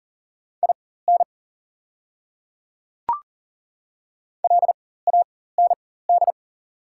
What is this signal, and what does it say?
Morse code, heard as a keyed tone, about 40 wpm, spaced as for 19 wpm, 690 Hz. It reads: IN LAND
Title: 4x Character Spacing - Top 200 Words - 40wpm